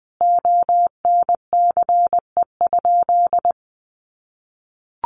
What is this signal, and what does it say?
Morse code, heard as a keyed tone, about 20 wpm, 700 Hz. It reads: ONCE?
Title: Sentences from Top 1100 Words 20wpm